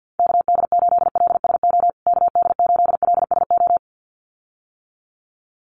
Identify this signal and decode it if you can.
Morse code, heard as a keyed tone, about 50 wpm, 700 Hz. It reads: KD8LSO KD8LSO
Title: USA Call Signs Repeated 2x 50wpm